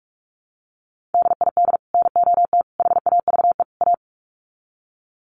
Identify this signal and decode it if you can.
Morse code, heard as a keyed tone, about 44 wpm, 700 Hz. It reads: DID NOT HAVE A
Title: Sets of 4 Words 45wpm